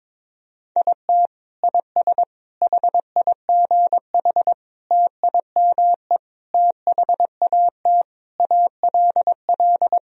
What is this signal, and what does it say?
Morse code, heard as a keyed tone, about 22 wpm, 700 Hz. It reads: IT IS HIGH TIME THAT ALL
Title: Sentences from Top 1000 Words - Review Part 1 - 22wpm